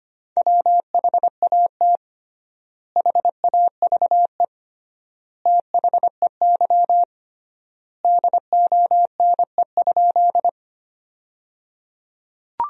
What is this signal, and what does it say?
Morse code, heard as a keyed tone, about 25 wpm, 700 Hz. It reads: WHAT HAVE THEY DONE?
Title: 3x Word-Spacing - Sentences from Top 500 Words - 25wpm